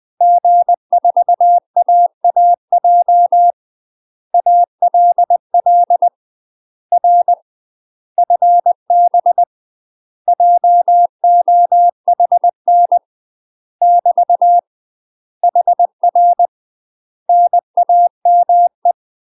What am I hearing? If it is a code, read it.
Morse code, heard as a keyed tone, about 20 wpm, 700 Hz. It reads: G4AAJ ALL R FB JOHN <BT> HR NAME
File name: UK MASTER QSOs TRAINING / 20WPM / UK MASTER QSOs 0051